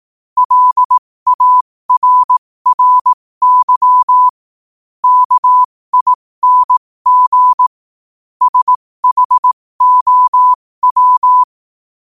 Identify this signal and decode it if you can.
Morse code, heard as a keyed tone, about 18 wpm, spaced as for 15 wpm, 1000 Hz. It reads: LARRY KING SHOW
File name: qotdfiles / 15wpm